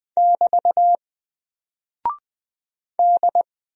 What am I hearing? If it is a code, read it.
Morse code, heard as a keyed tone, about 20 wpm, 700 Hz. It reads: <BT> D
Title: Single Character (Letter:Number:Punctuation:Prosign) 20wpm